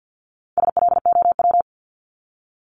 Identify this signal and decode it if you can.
Morse code, heard as a keyed tone, about 50 wpm, 700 Hz. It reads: SLOW
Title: Top 500 Words 50wpm